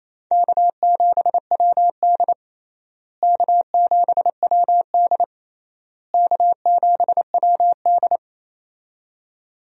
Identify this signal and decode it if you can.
Morse code, heard as a keyed tone, about 28 wpm, 700 Hz. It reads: K7WD K7WD K7WD
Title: USA Call Signs - 4 Characters - Repeated 3x - 28wpm